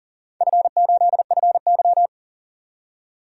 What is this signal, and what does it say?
Morse code, heard as a keyed tone, about 40 wpm, 700 Hz. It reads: F8FY
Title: DXCC-100 Call Signs - 4 Characters - 40wpm